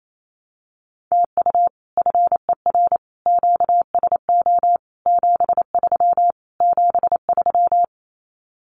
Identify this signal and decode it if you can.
Morse code, heard as a keyed tone, about 28 wpm, 700 Hz. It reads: TU FER QSO 73 73
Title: Common QSO Phrases Encore 28wpm